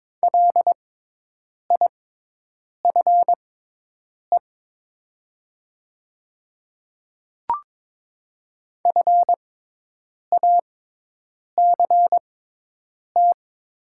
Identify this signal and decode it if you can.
Morse code, heard as a keyed tone, about 23 wpm, spaced as for 8 wpm, 700 Hz. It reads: LIFE FACT
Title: 6x Character Spacing - Top 500 Words - 22wpm